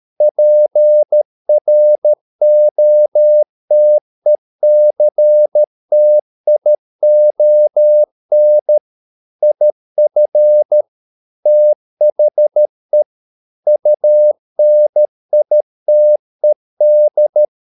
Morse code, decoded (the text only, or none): PROTECTION IF THE UNITED